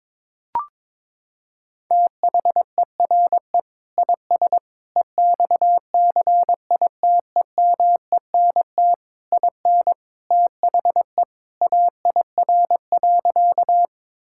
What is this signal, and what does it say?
Morse code, heard as a keyed tone, about 22 wpm, 700 Hz. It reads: THERE IS EXCITEMENT IN THE AIR.
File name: practice / 22wpm / Sentences with 3-Word Phrases - 22wpm